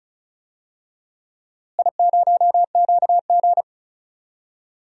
Morse code, heard as a keyed tone, about 34 wpm, 700 Hz. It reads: I0QG
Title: DXCC-100 Call Signs - 4 Characters - Repeated 2x - 35wpm